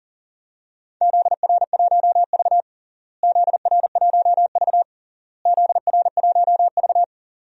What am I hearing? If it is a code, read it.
Morse code, heard as a keyed tone, about 40 wpm, 700 Hz. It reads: ZR1V ZR1V ZR1V
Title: DXCC-100 Call Signs - 4 Characters - Repeated 3x - 40wpm